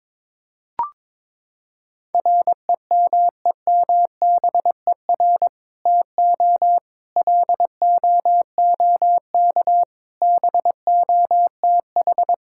Morse code, heard as a keyed tone, about 23 wpm, 700 Hz. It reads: REMEMBER TO LOOK BOTH
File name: practice / 22wpm / Sentences with 2-Word Phrases - 22wpm